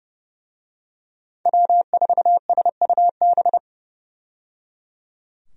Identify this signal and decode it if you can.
Morse code, heard as a keyed tone, about 30 wpm, 700 Hz. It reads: W4SUB
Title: USA Call Signs - 5 Characters - 30wpm